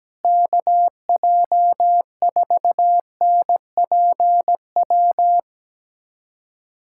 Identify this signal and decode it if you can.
Morse code, heard as a keyed tone, about 17 wpm, 700 Hz. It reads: KJ4NPW
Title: USA Callsign DE Callsign - 6 Characters - 17wpm